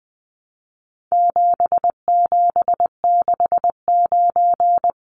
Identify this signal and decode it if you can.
Morse code, heard as a keyed tone, about 20 wpm, 700 Hz. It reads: 7769
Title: Numbers - 4-Digit 20wpm